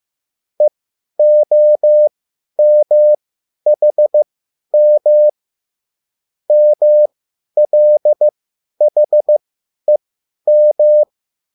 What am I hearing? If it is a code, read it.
Morse code, heard as a keyed tone, about 15 wpm, spaced as for 10 wpm, 600 Hz. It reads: EOMHM MLHEM
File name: QSL / CW lesson